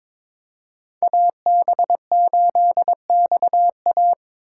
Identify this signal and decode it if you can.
Morse code, heard as a keyed tone, about 22 wpm, 700 Hz. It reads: AB8XA
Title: USA Call Signs - 8x Character Spacing - 22wpm